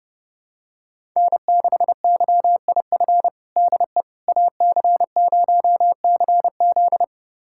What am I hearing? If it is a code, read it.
Morse code, heard as a keyed tone, about 32 wpm, 700 Hz. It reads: N6YIF DE AC0CZ
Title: USA Callsign DE Callsign - 5 Characters - 30wpm